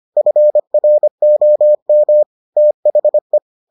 Morse code, heard as a keyed tone, about 25 wpm, 600 Hz. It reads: FROM THE